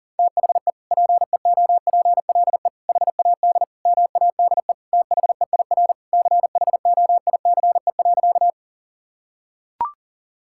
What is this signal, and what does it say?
Morse code, heard as a keyed tone, about 40 wpm, 700 Hz. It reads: THE PEOPLE HAD MADE THEIR CHOICE.
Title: Sentences from Top 1000 Words - Review Part 6 - 40wpm